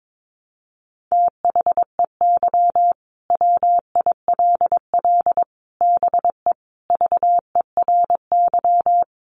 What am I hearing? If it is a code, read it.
Morse code, heard as a keyed tone, about 22 wpm, 700 Hz. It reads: THEY WILL BE VERY
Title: Sentences from Top 1100 Words 22wpm